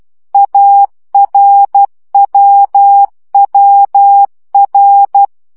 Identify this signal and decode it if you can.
Morse code, heard as a keyed tone, about 12 wpm, 800 Hz. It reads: ARWWR
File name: CW gyakorlas a r w 10 baud